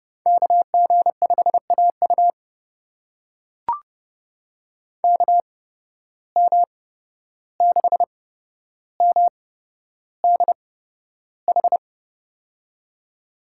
Morse code, heard as a keyed tone, about 32 wpm, spaced as for 12 wpm, 700 Hz. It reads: KG5AU KM6MDH
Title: USA Call Signs - 8x Character Spacing - 30wpm